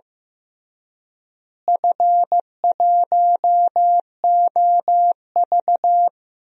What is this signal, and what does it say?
Morse code, heard as a keyed tone, about 15 wpm, 700 Hz. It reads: F1OV
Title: DXCC-100 Call Signs - 4 Characters - Repeated 3x - 15wpm